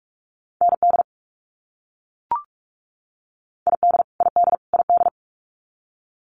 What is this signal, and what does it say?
Morse code, heard as a keyed tone, about 44 wpm, 700 Hz. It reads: ND ID ID ID
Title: US States Repeated 3x 45wpm